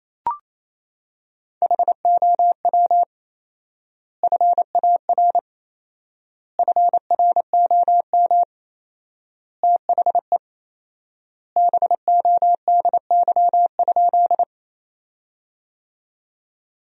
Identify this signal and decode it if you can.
Morse code, heard as a keyed tone, about 28 wpm, 700 Hz. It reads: HOW FAR FROM THE BODY?
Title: 4x Word-Spacing - Sentences from Top 400 Words - 28wpm